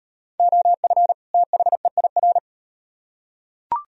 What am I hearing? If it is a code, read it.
Morse code, heard as a keyed tone, about 38 wpm, 700 Hz. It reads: OF THEIR